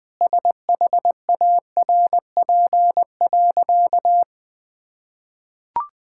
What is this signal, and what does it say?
Morse code, heard as a keyed tone, about 20 wpm, 700 Hz. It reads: SHARP.
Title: Sentences from Top 1600 Words 20wpm